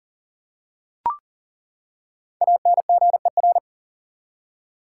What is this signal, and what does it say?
Morse code, heard as a keyed tone, about 40 wpm, 700 Hz. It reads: ANGER
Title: Five Letter Words - 40wpm